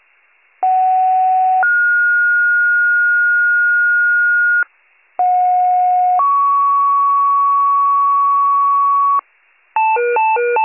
Two-tone page, heard as a one-second tone, then a three-second tone, about 750 then 1430 Hz, about 730 then 1120 Hz, with noise. It is heard from 0.6 to 9.2 s.